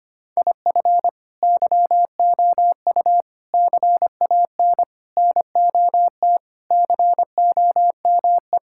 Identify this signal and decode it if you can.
Morse code, heard as a keyed tone, about 25 wpm, 700 Hz. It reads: IF YOU CAN NOT COME
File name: practice / 25wpm / Sentences with 4-Word Phrases - 25wpm